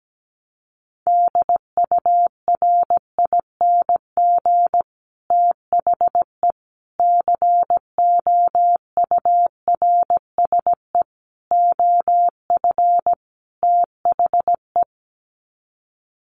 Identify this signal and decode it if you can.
Morse code, heard as a keyed tone, about 17 wpm, 690 Hz. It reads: DURING THE COURSE OF THE
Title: Sets of 5 Words Encore 17wpm